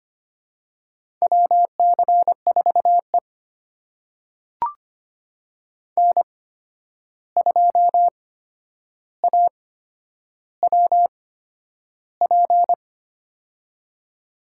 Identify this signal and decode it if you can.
Morse code, heard as a keyed tone, about 25 wpm, spaced as for 10 wpm, 700 Hz. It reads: WC4E N2AWP
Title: USA Call Signs - 8x Character Spacing - 25wpm